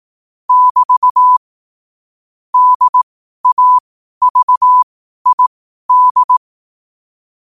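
Morse code, heard as a keyed tone, about 18 wpm, spaced as for 12 wpm, 1000 Hz. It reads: <BT> DAVID
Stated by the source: Morse code sent at 12 words per minute.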